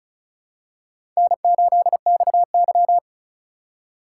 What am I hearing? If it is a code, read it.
Morse code, heard as a keyed tone, about 36 wpm, 700 Hz. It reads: N8XY
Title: USA Call Signs - 4x Character Spacing - 35wpm